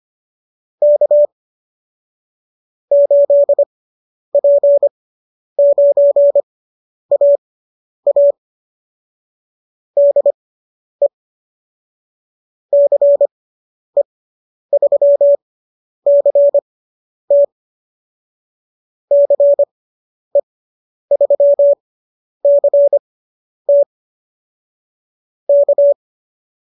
Morse code, heard as a keyed tone, about 25 wpm, spaced as for 10 wpm, 590 Hz. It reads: K 8P9AA DE CE3CT CE3CT K